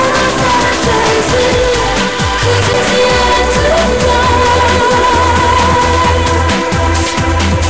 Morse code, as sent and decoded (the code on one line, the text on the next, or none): none